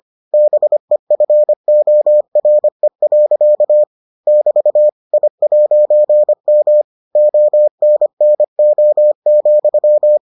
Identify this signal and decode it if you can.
Morse code, heard as a keyed tone, about 25 wpm, 600 Hz. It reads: BEFORE. <BT> I'M ONNO,